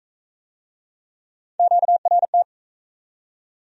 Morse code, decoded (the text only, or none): QRT